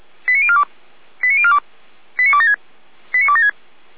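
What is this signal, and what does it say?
Five-tone selective call (ZVEI), heard as quick runs of five tones, tones of about 70 ms. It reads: 89042 89042 89276 89276
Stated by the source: ZVEI five-tone selective call received on an SDR.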